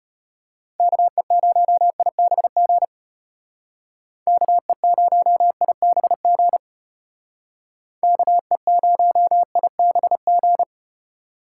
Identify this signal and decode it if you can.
Morse code, heard as a keyed tone, about 34 wpm, 700 Hz. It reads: KE0IBG KE0IBG KE0IBG